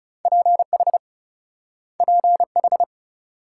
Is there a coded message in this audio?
Yes: Morse code from 0.2 to 2.8 s, heard as a keyed tone, about 38 wpm, 700 Hz. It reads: PH PH